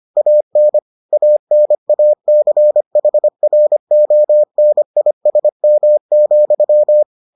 Morse code, heard as a keyed tone, about 25 wpm, 600 Hz. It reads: AN ANACHRONISM,